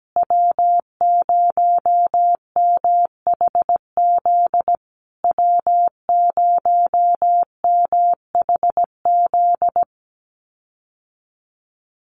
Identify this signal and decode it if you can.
Morse code, heard as a keyed tone, about 17 wpm, 700 Hz. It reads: W0MHZ W0MHZ
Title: USA Call Signs Repeated 3x 17wpm